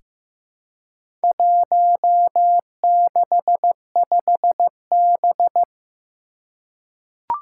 Morse code, heard as a keyed tone, about 15 wpm, 700 Hz. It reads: 165B
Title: Four Challenging Letter-Number Combinations 15wpm